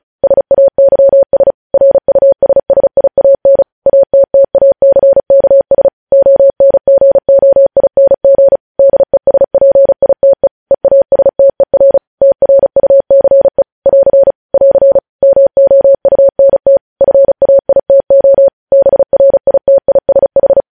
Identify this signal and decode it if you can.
Morse code, heard as a keyed tone, about 36 wpm, 550 Hz. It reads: SAYS RUSSIAN ATTACKS ONGOING DESPITE EASTER TRUCE <AR> <AR> MOUNT FAITO BRITISH
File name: podcast / News headlines-1x-35x35